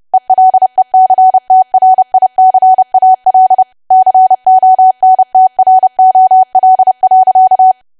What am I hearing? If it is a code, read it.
Morse code, heard as a keyed tone, about 30 wpm, 750 Hz. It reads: ELECTRICAL CONTROL.
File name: cwa / W1AW-Level3 / 30WPM